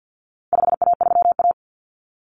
Morse code, heard as a keyed tone, about 50 wpm, 700 Hz. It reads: 5A3A
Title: Common Cut Numbers - 4-Digit 50wpm